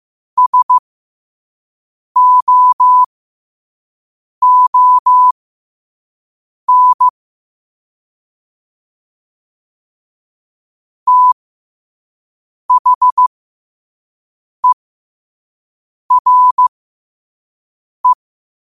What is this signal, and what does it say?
Morse code, heard as a keyed tone, about 15 wpm, spaced as for 5 wpm, 1000 Hz. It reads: SOON THERE
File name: qotdfiles / 5wpm